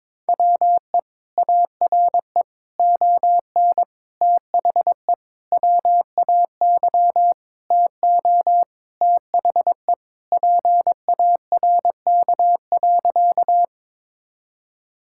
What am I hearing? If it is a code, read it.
Morse code, heard as a keyed tone, about 22 wpm, 700 Hz. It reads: WE ARE ON THE WAY TO THE PARK.
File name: practice / 22wpm / Sentences with 4-Word Phrases - 22wpm